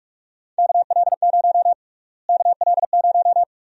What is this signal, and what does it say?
Morse code, heard as a keyed tone, about 46 wpm, 700 Hz. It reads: KL0 KL0
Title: USA Call Sign Prefixes - Repeated 2x - 45wpm